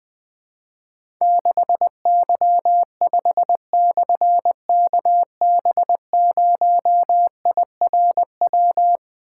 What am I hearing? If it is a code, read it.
Morse code, heard as a keyed tone, about 20 wpm, 700 Hz. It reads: 6Y5/KB0IRW
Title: Difficult Call Signs 20wpm